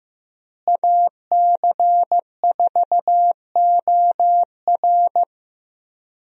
Morse code, heard as a keyed tone, about 15 wpm, 700 Hz. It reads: AC4OR
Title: USA Call Signs - 5 Characters - Repeated 3x - 15wpm